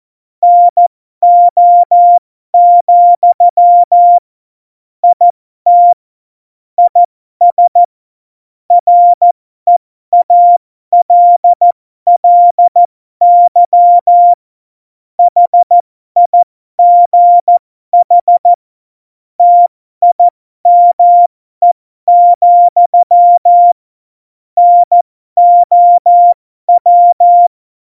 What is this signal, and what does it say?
Morse code, heard as a keyed tone, about 14 wpm, spaced as for 12 wpm, 700 Hz. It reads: NO, IT IS REALLY HIGH TIME, NOW
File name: Morsekurs / Witze